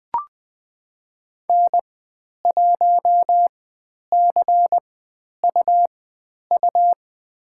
Morse code, heard as a keyed tone, about 20 wpm, spaced as for 10 wpm, 700 Hz. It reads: N1CUU
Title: 102 Lesson 17g - Letter u - Call Signs 20wpm